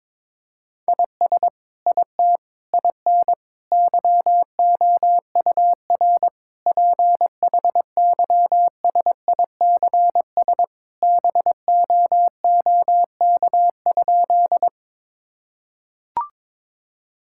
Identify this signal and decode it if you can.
Morse code, heard as a keyed tone, about 22 wpm, 700 Hz. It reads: IS IT IN YOUR PHYSICS BOOK?
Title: Sentences from Top 1000 Words - Review Part 5 - 22wpm